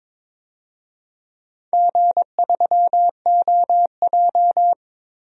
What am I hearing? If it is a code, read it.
Morse code, heard as a keyed tone, about 22 wpm, 700 Hz. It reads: G3OJ
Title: DXCC-100 Call Signs - 4 Characters - Repeated 2x - 22wpm